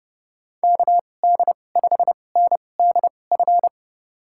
Morse code, heard as a keyed tone, about 30 wpm, spaced as for 22 wpm, 700 Hz. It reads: KD5NDF
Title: USA Call Signs - 6 Characters - 2x Character Spacing - 30wpm